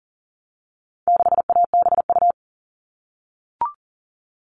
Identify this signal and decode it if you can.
Morse code, heard as a keyed tone, about 40 wpm, 700 Hz. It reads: 6ABU
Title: All Cut Numbers - 4-Digit 40wpm